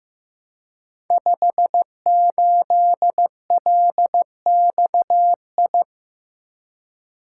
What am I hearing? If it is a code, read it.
Morse code, heard as a keyed tone, about 15 wpm, 700 Hz. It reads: 58LXI
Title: Code Groups - 5 Letter:Numbers - 15wpm